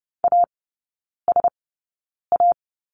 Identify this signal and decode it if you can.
Morse code, heard as a keyed tone, about 30 wpm, spaced as for 9 wpm, 700 Hz. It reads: ASA
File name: Lesson 19e - Letter w - Three Characters 30wpm